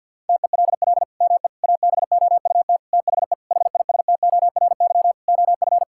Morse code, heard as a keyed tone, about 52 wpm, 700 Hz. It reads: TELL ME ABOUT THE HISTORY OF
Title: Sentences from Top 1000 Words - Review Part 6 - 50wpm